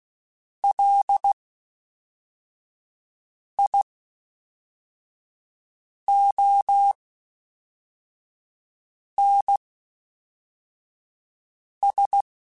Morse code, heard as a keyed tone, about 16 wpm, spaced as for 4 wpm, 800 Hz. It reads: LIONS
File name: Misc / Misc-mp3s / WordPairs sp @5